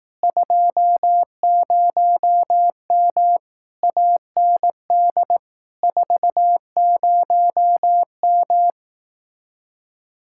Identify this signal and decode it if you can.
Morse code, heard as a keyed tone, about 18 wpm, 700 Hz. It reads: 20M AND 40M
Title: Speed-Racing Common QSO Phrases Encore - 15wpm